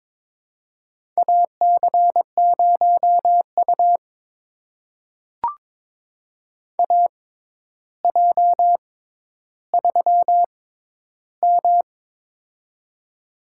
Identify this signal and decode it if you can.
Morse code, heard as a keyed tone, about 22 wpm, spaced as for 11 wpm, 700 Hz. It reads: AC0U AJ3M
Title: USA Call Signs - 4 Characters - 6x Character Spacing - 22wpm